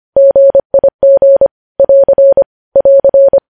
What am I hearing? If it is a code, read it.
Morse code, heard as a keyed tone, about 25 wpm, 550 Hz. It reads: GIG <AR> <AR>
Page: News Headlines in Morse Code at 25 WPM